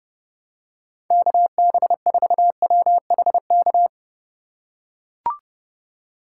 Morse code, heard as a keyed tone, about 30 wpm, 700 Hz. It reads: KB4WHK
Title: USA Call Signs - 6 Characters - 3x Character Spacing - 30wpm